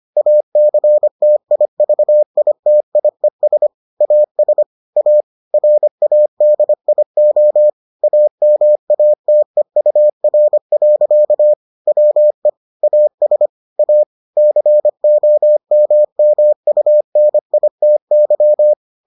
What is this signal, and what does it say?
Morse code, heard as a keyed tone, about 25 wpm, 600 Hz. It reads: ACTIVITIES AS A RADIO AMATEUR. WE AS A COMMUNITY